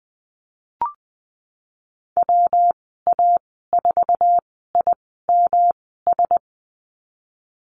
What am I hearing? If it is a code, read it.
Morse code, heard as a keyed tone, about 19 wpm, spaced as for 14 wpm, 700 Hz. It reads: WA4IMS
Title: 154 Lesson 25g - Letter p - Call Signs 20wpm